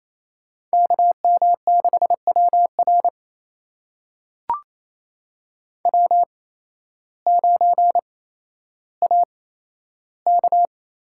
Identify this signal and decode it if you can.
Morse code, heard as a keyed tone, about 28 wpm, spaced as for 13 wpm, 700 Hz. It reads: KM6WR W9AK
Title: USA Call Signs - 5 Characters - 8x Character Spacing - 28wpm